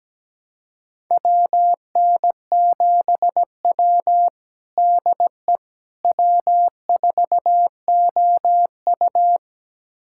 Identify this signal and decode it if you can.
Morse code, heard as a keyed tone, about 17 wpm, 700 Hz. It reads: WN7W DE W4OU
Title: USA Callsign DE Callsign - 4 Characters - 17wpm